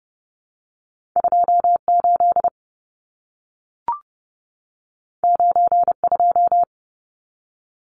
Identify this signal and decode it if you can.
Morse code, heard as a keyed tone, about 30 wpm, 700 Hz. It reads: 28 92
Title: Numbers - 2-Digit 30wpm